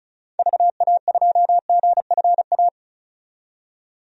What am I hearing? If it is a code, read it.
Morse code, heard as a keyed tone, about 36 wpm, 700 Hz. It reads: VA2GFA
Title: DXCC-100 Call Signs - 6 Characters - 35wpm